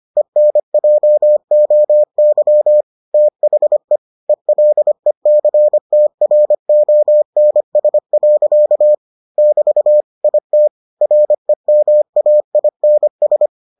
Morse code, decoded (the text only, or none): ENJOY THE ELECTRONS. <BT> IT REMAINS